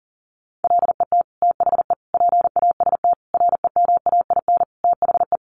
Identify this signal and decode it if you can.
Morse code, heard as a keyed tone, about 40 wpm, 700 Hz. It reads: LET THE PAST REMAIN THE
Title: Sentences from Top 1200 Words 40wpm